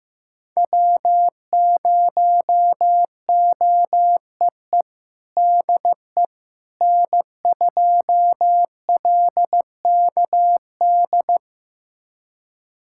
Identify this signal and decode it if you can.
Morse code, heard as a keyed tone, about 15 wpm, 700 Hz. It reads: W0OEE DE N2LKD
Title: USA Callsign DE Callsign - 5 Characters - 15wpm